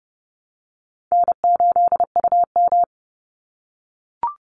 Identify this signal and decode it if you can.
Morse code, heard as a keyed tone, about 30 wpm, 700 Hz. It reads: N8UM
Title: Call Signs - 30wpm